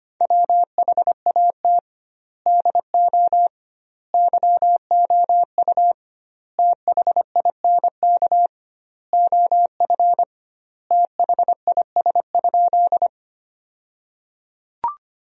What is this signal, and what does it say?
Morse code, heard as a keyed tone, about 25 wpm, 700 Hz. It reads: WHAT DO YOU THINK OF THIS?
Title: 2x Word-Spacing - Sentences from Top 200 Words - 25wpm